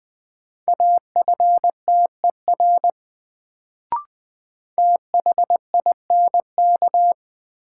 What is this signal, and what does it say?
Morse code, heard as a keyed tone, about 20 wpm, 700 Hz. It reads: AFTER THINK
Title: Sets of 2 Words - New Word Drill - 20wpm